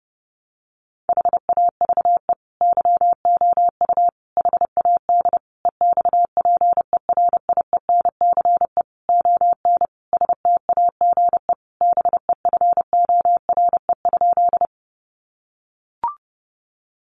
Morse code, heard as a keyed tone, about 30 wpm, 700 Hz. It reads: HAVE YOU HAD EXPERIENCE ON STAGE BEFORE?
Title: Sentences from Top 1200 Words 30wpm